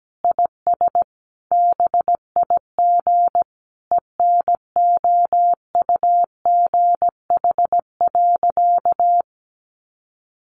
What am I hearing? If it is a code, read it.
Morse code, heard as a keyed tone, about 17 wpm, 700 Hz. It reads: IS BIG ENOUGH.
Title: Sentences from Top 400 Words 17wpm